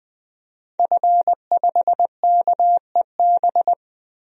Morse code, heard as a keyed tone, about 20 wpm, 700 Hz. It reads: F5KEB
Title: DXCC-100 Call Signs - 5 Characters - 20wpm